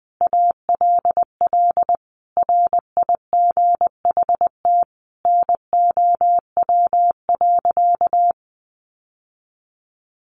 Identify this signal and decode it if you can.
Morse code, heard as a keyed tone, about 20 wpm, 700 Hz. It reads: ALL RIGHT NOW.
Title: Sentences from Top 700 Words 20wpm